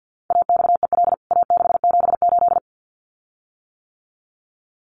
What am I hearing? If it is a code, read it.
Morse code, heard as a keyed tone, about 50 wpm, 700 Hz. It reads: AXEL A678
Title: CWT Exchange 50wpm